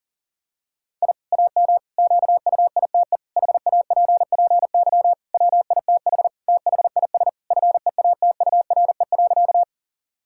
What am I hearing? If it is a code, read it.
Morse code, heard as a keyed tone, about 40 wpm, 700 Hz. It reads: I AM QUITE HAPPY WITH THIS FEATURE.
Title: Sentences from Top 1000 Words - Review Part 5 - 40wpm